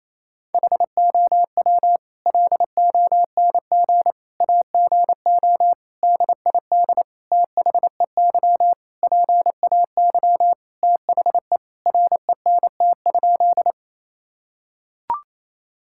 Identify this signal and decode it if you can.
Morse code, heard as a keyed tone, about 28 wpm, 700 Hz. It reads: HOW LONG AGO DID THEY PAY THE RENT?
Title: Sentences from Top 1000 Words - Review Part 5 - 28wpm